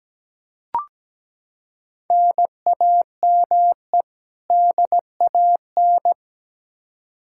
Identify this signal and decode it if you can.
Morse code, heard as a keyed tone, about 17 wpm, 700 Hz. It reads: NAME DAN
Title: Practice QSO - Exchange 1 - Beginning - 17wpm